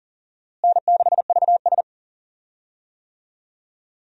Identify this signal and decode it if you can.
Morse code, heard as a keyed tone, about 40 wpm, 700 Hz. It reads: N6VS
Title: USA Call Signs - 4 Characters - Repeated 3x - 40wpm